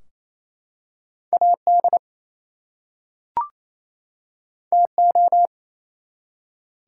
Morse code, heard as a keyed tone, about 27 wpm, 700 Hz. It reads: AD TO
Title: Two Letter Words - 28wpm